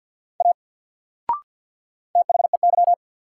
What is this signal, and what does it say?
Morse code, heard as a keyed tone, about 48 wpm, 700 Hz. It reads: A THEY
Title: Sentences from Top 100 Words - New Word Drill - 50wpm